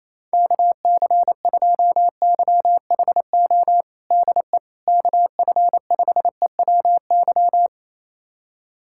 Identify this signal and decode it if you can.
Morse code, heard as a keyed tone, about 28 wpm, 700 Hz. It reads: KC2YHO DE KF5EWY
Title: USA Callsign DE Callsign - 6 Characters - 28wpm